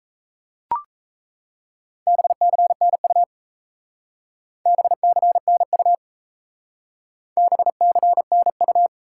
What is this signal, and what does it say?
Morse code, heard as a keyed tone, about 36 wpm, 700 Hz. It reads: BCNU BCNU BCNU